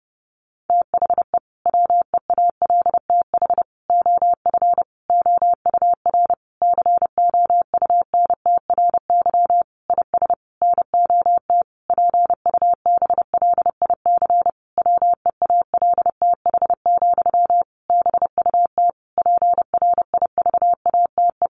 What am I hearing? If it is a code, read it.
Morse code, heard as a keyed tone, about 30 wpm, 700 Hz. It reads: THE WEALTH OF OUR COUNTRY IS NOT PUBLIC WEALTH, BUT PRIVATE
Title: Sentences from Top 1200 Words 30wpm